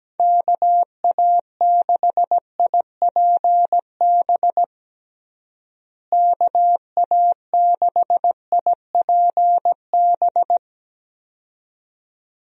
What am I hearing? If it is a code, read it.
Morse code, heard as a keyed tone, about 17 wpm, 700 Hz. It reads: KA6IPB KA6IPB
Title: USA Call Signs - 6 Characters - Repeated 2x - 17wpm